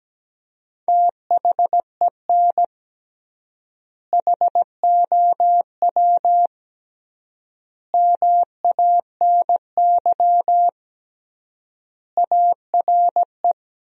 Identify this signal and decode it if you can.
Morse code, heard as a keyed tone, about 17 wpm, 700 Hz. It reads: THEN HOW MANY ARE
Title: 3x Word-Spacing - Sentences from Top 100 Words - 17wpm